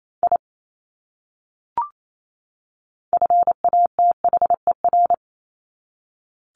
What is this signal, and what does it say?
Morse code, heard as a keyed tone, about 28 wpm, 700 Hz. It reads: I FATHER
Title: Top 200 Words 28wpm